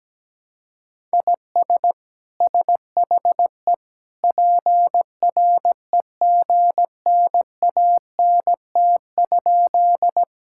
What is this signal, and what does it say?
Morse code, heard as a keyed tone, about 17 wpm, 700 Hz. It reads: IS SHE PREGNANT?
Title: Sentences from Top 1000 Words - Review Part 6 - 17wpm